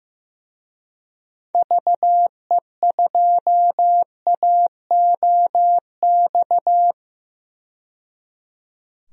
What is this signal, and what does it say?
Morse code, heard as a keyed tone, about 15 wpm, 700 Hz. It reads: VE2AOX
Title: DXCC-100 Call Signs - 6 Characters - 15wpm